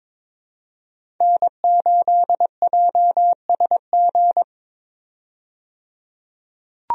Morse code, heard as a keyed tone, about 22 wpm, 700 Hz. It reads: N8JSG
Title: USA Call Signs - 5 Characters - 22wpm